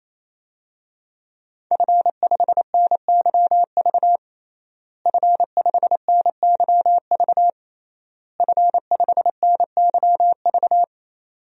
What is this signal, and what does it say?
Morse code, heard as a keyed tone, about 28 wpm, 700 Hz. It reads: F5NYV F5NYV F5NYV
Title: DXCC-100 Call Signs - 5 Characters - Repeated 3x - 28wpm